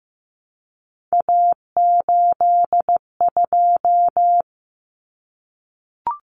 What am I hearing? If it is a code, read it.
Morse code, heard as a keyed tone, about 15 wpm, 700 Hz. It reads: A82
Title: Common Cut Numbers - 3-Digit 15wpm